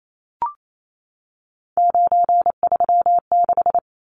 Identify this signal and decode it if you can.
Morse code, heard as a keyed tone, about 28 wpm, 700 Hz. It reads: 936
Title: Numbers - 3-Digit 28wpm